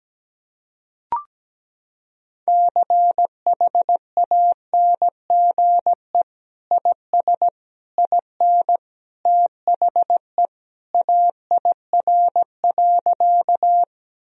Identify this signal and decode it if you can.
Morse code, heard as a keyed tone, about 17 wpm, 700 Hz. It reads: CHANGE IS IN THE AIR.
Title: Sentences from Top 1000 Words - Review Part 1 - 17wpm